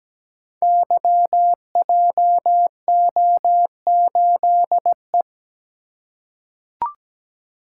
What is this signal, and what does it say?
Morse code, heard as a keyed tone, about 17 wpm, 700 Hz. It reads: YJO8E
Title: Code Groups - 5 Letter:Numbers - 17wpm